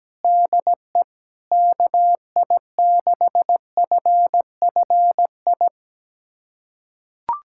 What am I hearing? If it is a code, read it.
Morse code, heard as a keyed tone, about 17 wpm, 700 Hz. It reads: DE KI6FFI
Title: USA Callsign DE Callsign - 6 Characters - 17wpm